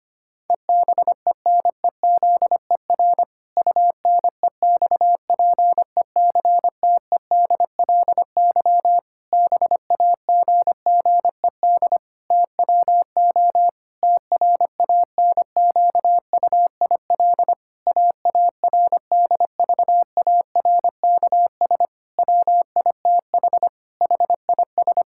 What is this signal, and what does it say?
Morse code, heard as a keyed tone, about 25 wpm, 700 Hz. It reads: EBENEZER UNEXPECTEDLY BAGGED TWO TRANQUIL AARDVARKS WITH HIS
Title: Pangram Challenge v2 - 25wpm